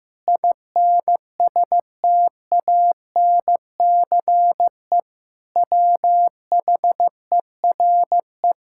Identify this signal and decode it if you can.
Morse code, heard as a keyed tone, about 15 wpm, 700 Hz. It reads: INSTANCE WHERE
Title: Sentences from Top 1000 Words - Review Part 5 - 15wpm